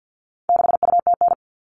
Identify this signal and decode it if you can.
Morse code, heard as a keyed tone, about 50 wpm, 700 Hz. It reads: 6UTN
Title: All Cut Numbers - 4-Digit 50wpm